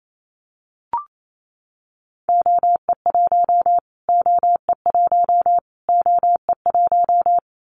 Morse code, heard as a keyed tone, about 28 wpm, 700 Hz. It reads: OE1 OE1 OE1
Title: Callsign Prefixes for Top 25 DXCC - Repeated 3x - 28wpm